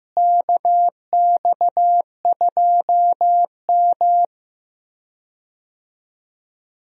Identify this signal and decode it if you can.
Morse code, heard as a keyed tone, about 15 wpm, 700 Hz. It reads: KX2M
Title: USA Call Signs - 4 Characters - Repeated 2x - 15wpm